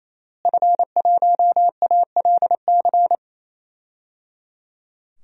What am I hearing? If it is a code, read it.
Morse code, heard as a keyed tone, about 28 wpm, 690 Hz. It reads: F1ALC
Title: DXCC-100 Call Signs - 5 Characters - Repeated 3x - 28wpm